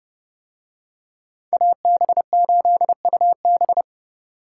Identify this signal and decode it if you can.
Morse code, heard as a keyed tone, about 30 wpm, 700 Hz. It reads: AB8UB